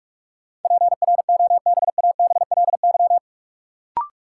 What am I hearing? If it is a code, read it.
Morse code, heard as a keyed tone, about 46 wpm, 700 Hz. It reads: PROBABLY
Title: Sentences from Top 700 Words - New Word Drill - 45wpm